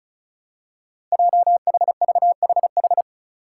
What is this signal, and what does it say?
Morse code, heard as a keyed tone, about 36 wpm, 700 Hz. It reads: JHVHH
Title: Five Challenging Letter-Number Combinations 35wpm